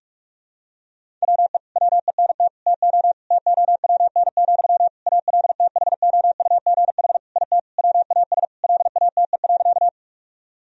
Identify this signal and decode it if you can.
Morse code, heard as a keyed tone, about 45 wpm, 700 Hz. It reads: WE WENT TO TOWN, ALTHOUGH IT WAS LATE.
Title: Sentences from Top 1000 Words - Review Part 3 - 45wpm